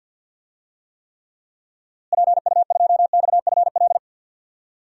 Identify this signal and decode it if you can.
Morse code, heard as a keyed tone, about 52 wpm, 700 Hz. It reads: PU2XFL
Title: DXCC-100 Call Signs - 6 Characters - Repeated 2x - 50wpm